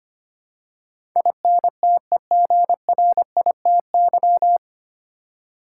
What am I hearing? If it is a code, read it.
Morse code, heard as a keyed tone, about 25 wpm, 700 Hz. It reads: INTEGRITY
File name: Nine Letter Words - 25wpm